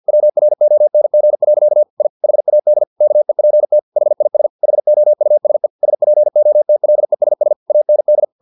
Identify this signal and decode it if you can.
Morse code, heard as a keyed tone, about 50 wpm, 600 Hz. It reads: WRONG. I HAD KEPT HIS HOUSE SPOTLESS AND
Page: Never 49wpm